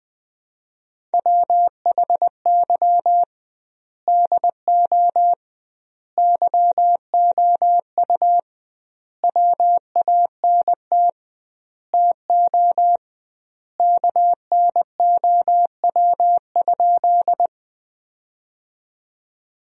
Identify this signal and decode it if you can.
Morse code, heard as a keyed tone, about 21 wpm, 700 Hz. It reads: WHY DO YOU WANT TO KNOW?
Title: 2x Word-Spacing - Sentences from Top 200 Words - 20wpm